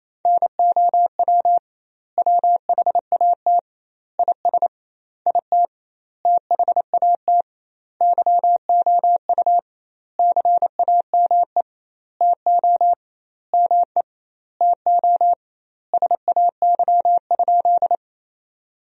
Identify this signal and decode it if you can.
Morse code, heard as a keyed tone, about 28 wpm, 700 Hz. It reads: NOW WHAT IS IT THAT YOU CAME TO ME TO SAY?
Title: 2x Word-Spacing - Sentences from Top 200 Words - 28wpm